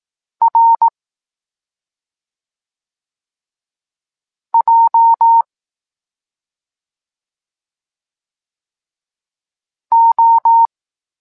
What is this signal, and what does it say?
Morse code, heard as a keyed tone, about 18 wpm, 920 Hz. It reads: R J O